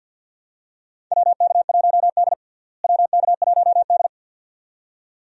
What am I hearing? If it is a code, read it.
Morse code, heard as a keyed tone, about 53 wpm, 700 Hz. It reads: WK1D WK1D
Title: USA Call Signs - 4 Characters - Repeated 2x - 50wpm